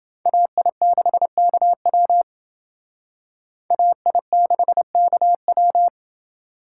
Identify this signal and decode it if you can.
Morse code, heard as a keyed tone, about 30 wpm, 700 Hz. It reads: AI6KW AI6KW